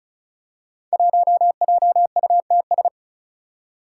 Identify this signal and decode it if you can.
Morse code, heard as a keyed tone, about 36 wpm, 700 Hz. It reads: 1JUTS
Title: Five Challenging Letter-Number Combinations 35wpm